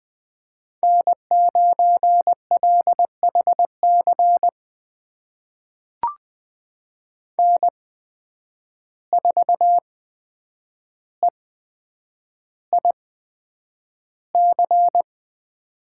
Morse code, heard as a keyed tone, about 20 wpm, spaced as for 9 wpm, 700 Hz. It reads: N9LHC N4EIC
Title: USA Call Signs - 5 Characters - 8x Character Spacing - 20wpm